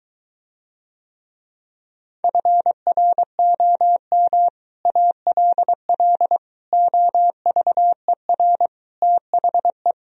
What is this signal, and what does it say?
Morse code, heard as a keyed tone, about 23 wpm, 700 Hz. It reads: FROM ALL OVER THE